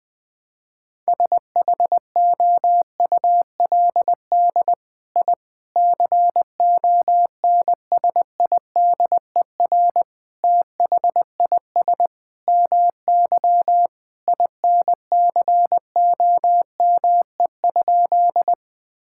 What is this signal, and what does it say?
Morse code, heard as a keyed tone, about 21 wpm, 700 Hz. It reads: SHOULD I CONSIDER THIS MY INCOME?
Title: Sentences from Top 1000 Words - Review Part 5 - 20wpm